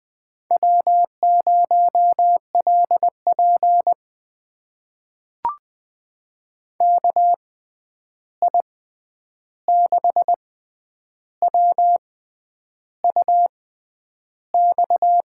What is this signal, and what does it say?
Morse code, heard as a keyed tone, about 21 wpm, spaced as for 9 wpm, 700 Hz. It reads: W0LP KI6WUX
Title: USA Call Signs - 6x Character Spacing - 20wpm